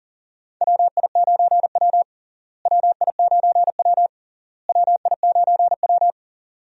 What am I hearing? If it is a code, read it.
Morse code, heard as a keyed tone, about 40 wpm, 700 Hz. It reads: WI9W WI9W WI9W
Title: USA Call Signs - 4 Characters - Repeated 3x - 40wpm